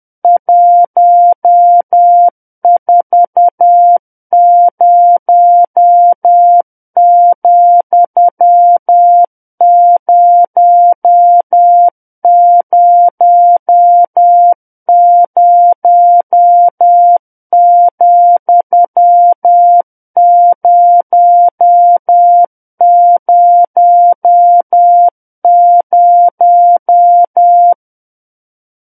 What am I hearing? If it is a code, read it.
Morse code, heard as a keyed tone, about 10 wpm, 700 Hz. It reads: 140,000,000
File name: Book One / 01-Chapter 1 - 10 WPM